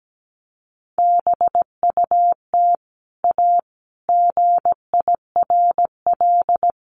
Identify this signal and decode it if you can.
Morse code, heard as a keyed tone, about 17 wpm, 700 Hz. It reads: BUT A GIRL
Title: Sentences from Top 1200 Words 17wpm